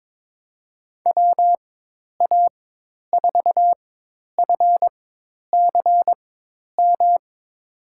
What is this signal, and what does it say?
Morse code, heard as a keyed tone, about 22 wpm, spaced as for 10 wpm, 700 Hz. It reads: WA4FCM
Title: USA Call Signs - 6 Characters - 4x Character Spacing - 22wpm